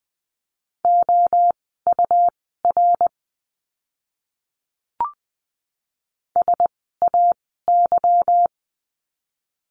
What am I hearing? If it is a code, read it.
Morse code, heard as a keyed tone, about 20 wpm, spaced as for 15 wpm, 700 Hz. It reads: OUR SAY
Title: 160 Lesson 26f - Letter g - Words 20wpm